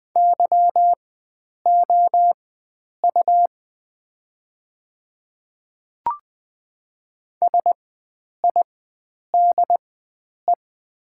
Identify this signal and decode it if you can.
Morse code, heard as a keyed tone, about 20 wpm, spaced as for 9 wpm, 700 Hz. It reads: YOU SIDE
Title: 4x Character Spacing - Top 100 Words - 20wpm